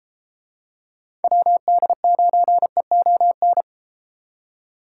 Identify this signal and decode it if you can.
Morse code, heard as a keyed tone, about 31 wpm, 700 Hz. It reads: WD9EON